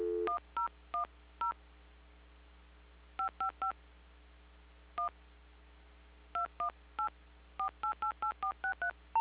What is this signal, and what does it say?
DTMF, heard as telephone touch tones, tones of about 110 ms. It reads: 1010 555 1 218 4888463
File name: recordings / PortlandUnion Millennium